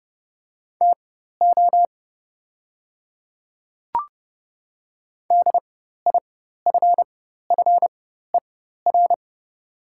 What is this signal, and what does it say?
Morse code, heard as a keyed tone, about 30 wpm, spaced as for 14 wpm, 700 Hz. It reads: TO DIFFER
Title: 4x Character Spacing - Top 500 Words - 30wpm